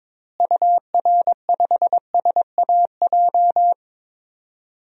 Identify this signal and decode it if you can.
Morse code, heard as a keyed tone, about 22 wpm, 700 Hz. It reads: UR5SAJ
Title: DXCC-100 Call Signs - 6 Characters - Repeated 3x - 22wpm